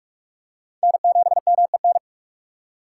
Morse code, heard as a keyed tone, about 46 wpm, 700 Hz. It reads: N7MEN
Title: USA Call Signs - 4x Character Spacing - 45wpm